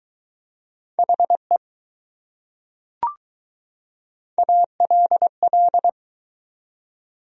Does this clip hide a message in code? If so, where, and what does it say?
Yes: Morse code from 1.0 to 5.9 s, heard as a keyed tone, about 20 wpm, 700 Hz. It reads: HE ALL